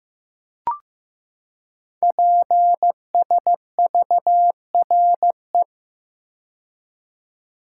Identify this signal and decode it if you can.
Morse code, heard as a keyed tone, about 15 wpm, 700 Hz. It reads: PSVRE
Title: Code Groups - 5 Letter:Numbers - 15wpm